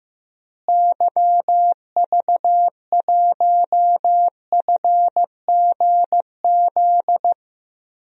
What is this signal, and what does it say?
Morse code, heard as a keyed tone, about 15 wpm, 700 Hz. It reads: YV1FGZ
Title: DXCC-100 Call Signs - 6 Characters - Repeated 3x - 15wpm